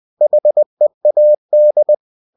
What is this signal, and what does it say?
Morse code, heard as a keyed tone, about 20 wpm, 600 Hz. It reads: HEAD
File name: cw-books / winnie-the-pooh / winnie-20wpm-page-0014